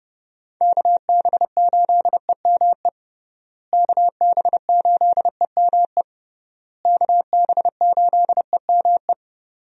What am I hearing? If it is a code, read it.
Morse code, heard as a keyed tone, about 30 wpm, 700 Hz. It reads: KB8EME KB8EME KB8EME
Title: USA Call Signs - 6 Characters - Repeated 3x - 30wpm